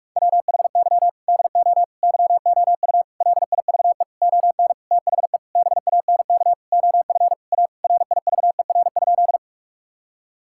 Morse code, heard as a keyed tone, about 44 wpm, 700 Hz. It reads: WHY DO YOU LIVE ON THE BANK OF A RIVER?
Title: Sentences from Top 1000 Words - Review Part 6 - 45wpm